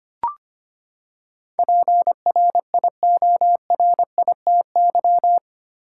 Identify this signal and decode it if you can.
Morse code, heard as a keyed tone, about 25 wpm, 700 Hz. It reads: PRIORITY
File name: Eight Letter Words - 25wpm